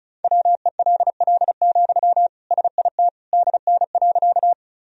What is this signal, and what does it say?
Morse code, heard as a keyed tone, about 36 wpm, 700 Hz. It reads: WELL, SIT DN.
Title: Sentences from Top 500 Words and Abbreviations 35wpm